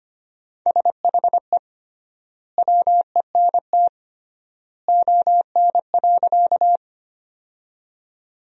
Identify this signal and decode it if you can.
Morse code, heard as a keyed tone, about 25 wpm, 700 Hz. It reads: SHE WENT ON.
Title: 3x Word-Spacing - Sentences from Top 200 Words - 25wpm